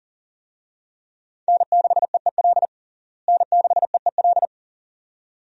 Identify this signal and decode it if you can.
Morse code, heard as a keyed tone, about 40 wpm, 700 Hz. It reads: N6EEL N6EEL
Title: USA Call Signs - 5 Characters - Repeated 2x - 40wpm